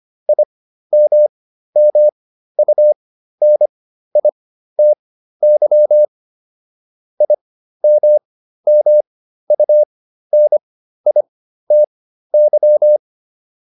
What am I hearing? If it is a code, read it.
Morse code, heard as a keyed tone, about 25 wpm, spaced as for 13 wpm, 600 Hz. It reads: IMMUNITY IMMUNITY